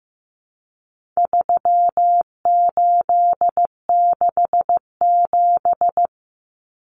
Morse code, heard as a keyed tone, about 15 wpm, 700 Hz. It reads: 3867
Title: Numbers - 4-Digit 15wpm